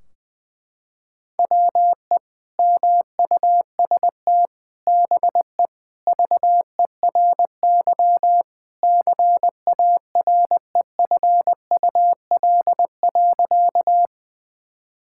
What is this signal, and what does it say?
Morse code, heard as a keyed tone, about 20 wpm, 700 Hz. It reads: WE MUST BE VERY CAREFUL.
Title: Sentences from Top 1600 Words 20wpm